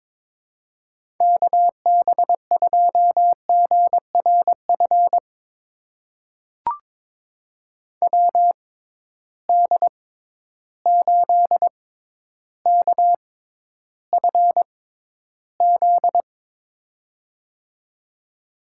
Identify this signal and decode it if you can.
Morse code, heard as a keyed tone, about 22 wpm, spaced as for 11 wpm, 700 Hz. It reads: KB2GRF WD8KFZ
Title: USA Call Signs - 6 Characters - 6x Character Spacing - 22wpm